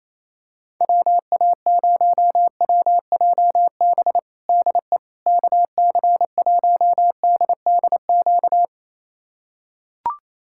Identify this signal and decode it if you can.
Morse code, heard as a keyed tone, about 28 wpm, 700 Hz. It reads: WA0WJB DE KC1DDQ
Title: USA Callsign DE Callsign - 6 Characters - 28wpm